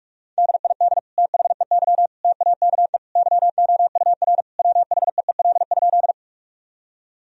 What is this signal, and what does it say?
Morse code, heard as a keyed tone, about 46 wpm, 700 Hz. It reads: DID THEY TAKE YOUR WHEEL?
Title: Sentences from Top 1000 Words - Review Part 3 - 45wpm